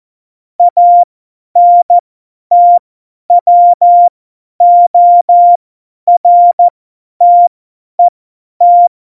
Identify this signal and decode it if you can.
Morse code, heard as a keyed tone, about 14 wpm, spaced as for 10 wpm, 700 Hz. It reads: ANTWORTET
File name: Morsekurs / Witze / Witz 02. April 2019, 50 F70